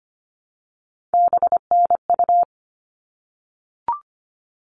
Morse code, heard as a keyed tone, about 25 wpm, 700 Hz. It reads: BNU
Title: All Cut Numbers - 3-Digit 25wpm